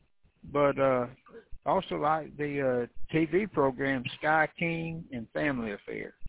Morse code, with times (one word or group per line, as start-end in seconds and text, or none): none